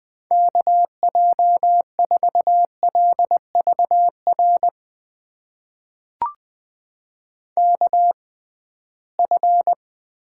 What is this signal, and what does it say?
Morse code, heard as a keyed tone, about 20 wpm, spaced as for 15 wpm, 700 Hz. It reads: KJ4LVR KF
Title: USA Call Signs - 6 Characters - 6x Character Spacing - 20wpm